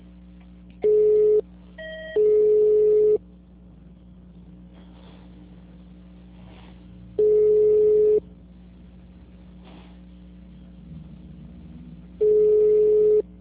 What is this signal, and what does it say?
Ringback tone (Europe, 425 Hz).
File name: out- -2029- - - .75